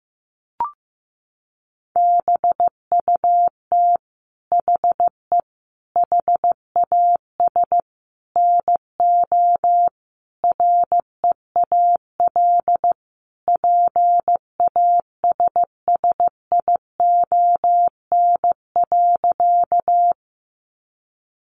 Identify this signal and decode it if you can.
Morse code, heard as a keyed tone, about 15 wpm, 700 Hz. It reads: BUT HE HAS NO REAL PASSION.
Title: Sentences from Top 1300 Words 15wpm